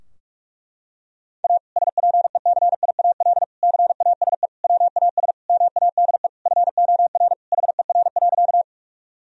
Morse code, heard as a keyed tone, about 41 wpm, 700 Hz. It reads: A SPECIAL CASE WAS MADE FOR HER.
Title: Sentences from Top 1000 Words - Review Part 3 - 45wpm